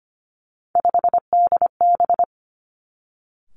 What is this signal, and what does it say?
Morse code, heard as a keyed tone, about 25 wpm, 700 Hz. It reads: 5DB
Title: Three Challenging Letter-Number Combinations 25wpm